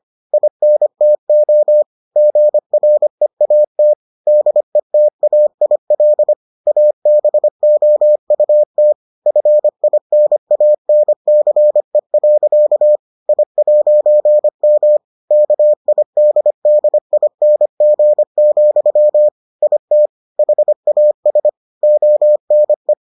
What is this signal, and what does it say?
Morse code, heard as a keyed tone, about 25 wpm, 600 Hz. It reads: INTO GREAT DETAIL ABOUT FINANCE. I'M KIDDING, IT HAS ONE